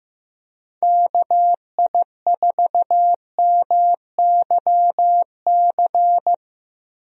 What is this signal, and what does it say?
Morse code, heard as a keyed tone, about 15 wpm, 700 Hz. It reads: KI4MYC
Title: USA Call Signs - 6 Characters - Repeated 3x - 15wpm